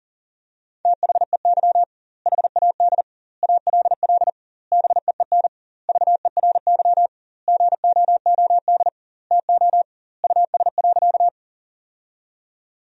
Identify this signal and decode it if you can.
Morse code, heard as a keyed tone, about 37 wpm, 700 Hz. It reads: THEY HAD ALL BEEN VERY GOOD TO US.
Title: 2x Word-Spacing - Sentences from Top 200 Words - 40wpm